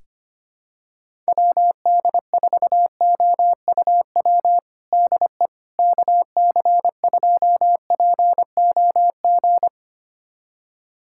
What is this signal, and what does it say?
Morse code, heard as a keyed tone, about 25 wpm, 700 Hz. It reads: WD4OUW DE KC2POG
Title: USA Callsign DE Callsign 25wpm